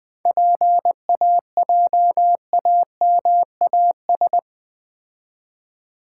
Morse code, heard as a keyed tone, about 24 wpm, 700 Hz. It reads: PAJAMAS
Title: Easy Words - Speed-Racing with Hints - 20wpm